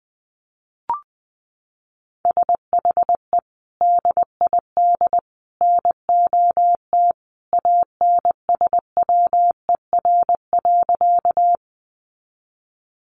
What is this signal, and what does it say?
Morse code, heard as a keyed tone, about 20 wpm, 700 Hz. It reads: SHE DID NOT ANSWER.
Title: Sentences from Top 600 Words 20wpm